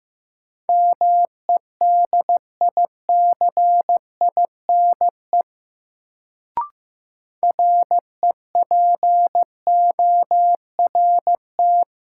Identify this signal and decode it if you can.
Morse code, heard as a keyed tone, about 15 wpm, 700 Hz. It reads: MEDICINE REPORT
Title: Sentences from Top 900 Words - New Word Drill - 15wpm